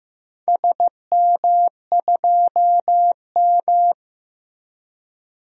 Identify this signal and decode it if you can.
Morse code, heard as a keyed tone, about 16 wpm, 700 Hz. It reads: SM2M
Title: DXCC-100 Call Signs - 4 Characters - Repeated 2x - 15wpm